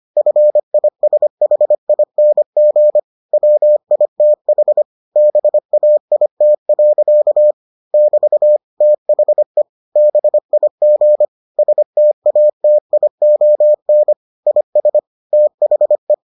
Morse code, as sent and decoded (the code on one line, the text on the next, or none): ..-. .. ... .... .. -. --. / .-- .. - .... / -... .- .. - .-.-.- / -...- / - .... / -... .. --. / ... - .- - .. --- -. / .. ... / - .... .
FISHING WITH BAIT. <BT> THE BIG STATION IS THE